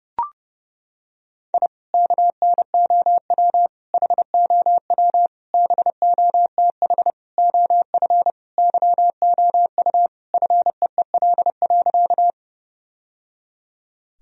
Morse code, heard as a keyed tone, about 30 wpm, 700 Hz. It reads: I KNOW HOW BOTH OF YOU FEEL.
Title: Sentences from Top 1000 Words - Review Part 1 - 30wpm